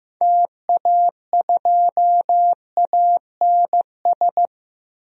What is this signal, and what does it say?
Morse code, heard as a keyed tone, about 15 wpm, 700 Hz. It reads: TA2ANS
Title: DXCC-100 Call Signs - 6 Characters - Repeated 3x - 15wpm